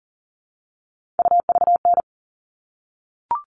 Morse code, heard as a keyed tone, about 40 wpm, 700 Hz. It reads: UVN
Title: All Cut Numbers - 3-Digit 40wpm